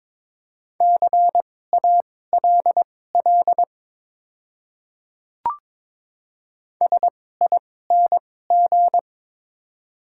Morse code, heard as a keyed tone, about 22 wpm, spaced as for 16 wpm, 700 Hz. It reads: CALL SING
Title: 2x Character Spacing - Top 500 Words - 22wpm